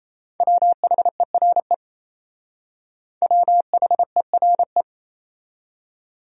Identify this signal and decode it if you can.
Morse code, heard as a keyed tone, about 35 wpm, 700 Hz. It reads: WHERE WHERE
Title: Speed-Racing Top 200 Words 28wpm